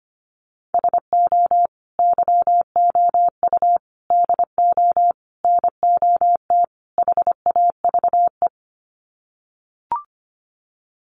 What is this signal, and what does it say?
Morse code, heard as a keyed tone, about 25 wpm, 700 Hz. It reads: SO YOU DO NOT HAVE
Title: Sets of 5 Words Encore 25wpm